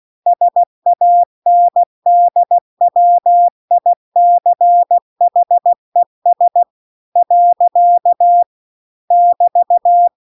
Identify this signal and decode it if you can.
Morse code, heard as a keyed tone, about 16 wpm, 700 Hz. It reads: SANDWICHES . <BT>
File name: CW-The-Big-Twoheaded-River-2-16WPM